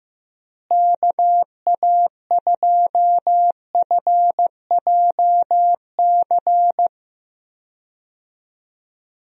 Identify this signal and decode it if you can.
Morse code, heard as a keyed tone, about 15 wpm, 700 Hz. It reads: KA2FJC
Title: USA Call Signs - 6 Characters - Repeated 2x - 15wpm